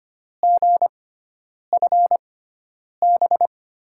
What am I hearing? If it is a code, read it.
Morse code, heard as a keyed tone, about 25 wpm, spaced as for 9 wpm, 700 Hz. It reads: GFB
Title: USA Call Signs - 6 Characters - 6x Character Spacing - 25wpm